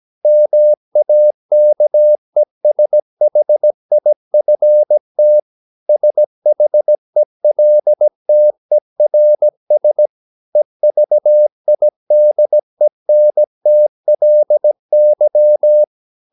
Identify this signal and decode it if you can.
Morse code, heard as a keyed tone, about 17 wpm, 600 Hz. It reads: MAKESHIFT SHELTERS EVIDENTLY
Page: Romanen Never af Ken Follet på engelsk.